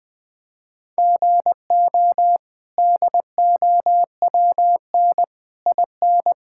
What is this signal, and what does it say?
Morse code, heard as a keyed tone, about 20 wpm, 700 Hz. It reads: GO DOWN IN
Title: Sets of 8 Words - 20wpm